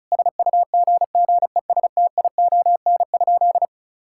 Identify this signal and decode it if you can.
Morse code, heard as a keyed tone, about 36 wpm, 700 Hz. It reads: SUGGESTION?
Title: Sentences from Top 1600 Words 35wpm